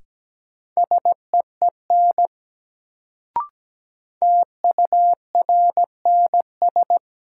Morse code, heard as a keyed tone, about 17 wpm, 700 Hz. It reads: SEEN TURNS
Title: Sets of 5 Words Encore - New Word Drill - 17wpm